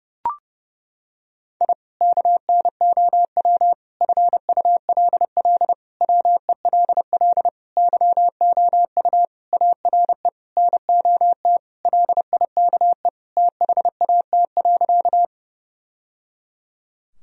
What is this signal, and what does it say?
Morse code, heard as a keyed tone, about 30 wpm, 700 Hz. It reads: I KNOW FULL WELL YOU ARE NOT LIKE THAT.
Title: Sentences from Top 1000 Words - Review Part 3 - 30wpm